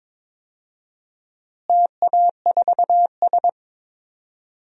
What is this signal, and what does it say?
Morse code, heard as a keyed tone, about 22 wpm, 700 Hz. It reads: TA4S
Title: DXCC-100 Call Signs - 4 Characters - Repeated 3x - 22wpm